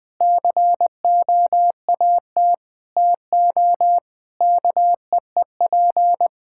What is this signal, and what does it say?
Morse code, heard as a keyed tone, about 20 wpm, 700 Hz. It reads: COAT TO KEEP
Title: Difficult Sentences with Hints - 20wpm